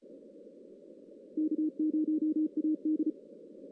Morse code, heard as a keyed tone, about 34 wpm, 310 Hz. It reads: K0AD
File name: 2016 JAN CW NAQP / 2016 Jan naqp k5ot - 0-11